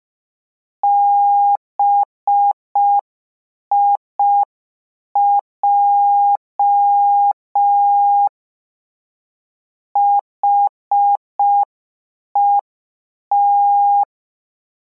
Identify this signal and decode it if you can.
Morse code, heard as a keyed tone, about 5 wpm, 810 Hz. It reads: BIJ HET